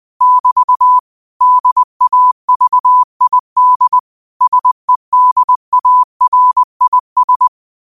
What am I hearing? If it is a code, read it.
Morse code, heard as a keyed tone, about 20 wpm, 1000 Hz. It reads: <BT> DAVID SEDARIS